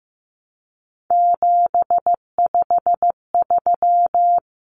Morse code, heard as a keyed tone, about 15 wpm, 700 Hz. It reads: 753
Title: Common Cut Numbers - 3-Digit 15wpm